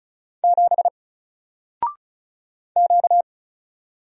Morse code, heard as a keyed tone, about 36 wpm, 700 Hz. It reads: 7 Q
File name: Single-Letter-Number-35wpm